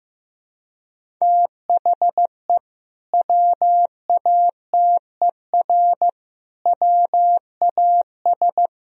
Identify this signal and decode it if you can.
Morse code, heard as a keyed tone, about 15 wpm, 700 Hz. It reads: THE WATER WAS
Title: Sentences from Top 1000 Words - Review Part 3 - 15wpm